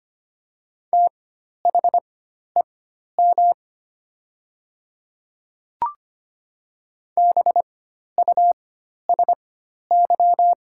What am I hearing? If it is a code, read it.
Morse code, heard as a keyed tone, about 25 wpm, spaced as for 12 wpm, 700 Hz. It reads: THEM BUSY
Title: 4x Character Spacing - Top 500 Words - 25wpm